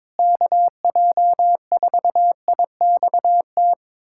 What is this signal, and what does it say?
Morse code, heard as a keyed tone, about 22 wpm, 700 Hz. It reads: KJ4IXT
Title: USA Callsign DE Callsign - 6 Characters - 22wpm